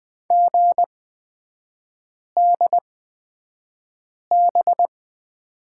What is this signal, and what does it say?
Morse code, heard as a keyed tone, about 20 wpm, spaced as for 5 wpm, 700 Hz. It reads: GDB
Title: 250 Lesson 40e - BK - Three Characters 20wpm